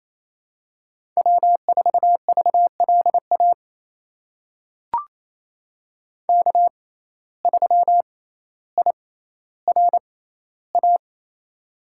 Morse code, heard as a keyed tone, about 28 wpm, spaced as for 14 wpm, 700 Hz. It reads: W4VLA K3IRA
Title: USA Call Signs - 5 Characters - 6x Character Spacing - 28wpm